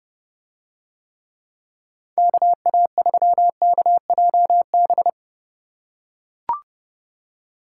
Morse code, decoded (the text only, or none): KA3KJB